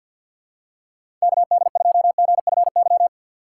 Morse code, heard as a keyed tone, about 48 wpm, 700 Hz. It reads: KD2GFY
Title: USA Call Signs - 6 Characters - 4x Character Spacing - 50wpm